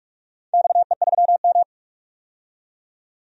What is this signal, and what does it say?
Morse code, heard as a keyed tone, about 44 wpm, 700 Hz. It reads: XE3M
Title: DXCC-100 Call Signs - 4 Characters - 45wpm